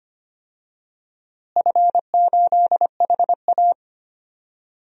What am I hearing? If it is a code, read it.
Morse code, heard as a keyed tone, about 25 wpm, 700 Hz. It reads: F8HA
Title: DXCC-100 Call Signs - 4 Characters - Repeated 2x - 25wpm